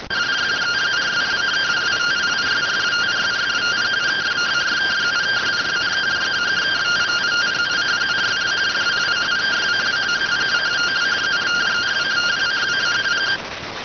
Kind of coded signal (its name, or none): two-tone teleprinter signal